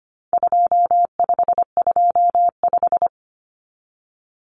Morse code, heard as a keyed tone, about 25 wpm, 700 Hz. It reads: 2525
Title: Common Cut Numbers - 4-Digit 25wpm